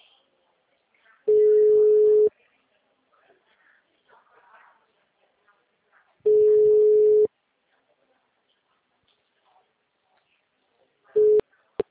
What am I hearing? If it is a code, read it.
Ringback tone (Europe, 425 Hz).